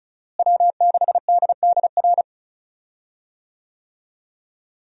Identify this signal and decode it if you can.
Morse code, heard as a keyed tone, about 36 wpm, 700 Hz. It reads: W6DDR
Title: USA Call Signs - 5 Characters - Repeated 3x - 35wpm